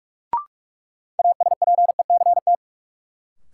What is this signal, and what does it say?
Morse code, heard as a keyed tone, about 42 wpm, 700 Hz. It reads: ASPECT
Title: Sentences from Top 1300 Words - New Word Drill Alt - 45wpm